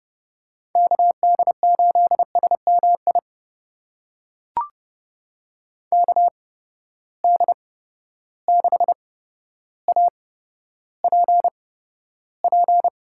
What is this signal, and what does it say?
Morse code, heard as a keyed tone, about 30 wpm, spaced as for 13 wpm, 700 Hz. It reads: KD8SMI KD6APP
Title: USA Call Signs - 6 Characters - 8x Character Spacing - 30wpm